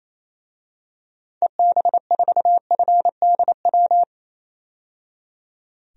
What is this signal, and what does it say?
Morse code, heard as a keyed tone, about 27 wpm, 700 Hz. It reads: EB4FDW
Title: DXCC-100 Call Signs - 6 Characters - 28wpm